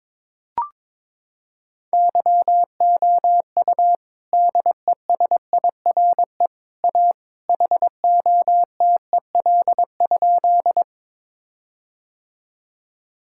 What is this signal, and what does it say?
Morse code, heard as a keyed tone, about 22 wpm, 700 Hz. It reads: YOU DESIRE A HOTEL?
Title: Sentences from Top 1600 Words 22wpm